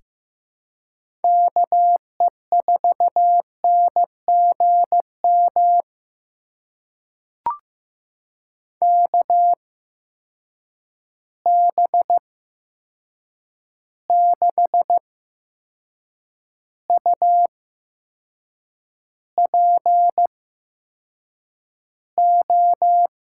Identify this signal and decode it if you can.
Morse code, heard as a keyed tone, about 15 wpm, spaced as for 6 wpm, 700 Hz. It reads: KE4NGM KB6UPO
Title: USA Call Signs - 6 Characters - 8x Character Spacing - 15wpm